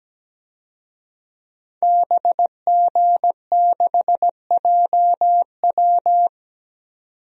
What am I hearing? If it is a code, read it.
Morse code, heard as a keyed tone, about 17 wpm, 700 Hz. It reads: BG6JW
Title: DXCC-100 Call Signs - 5 Characters - 17wpm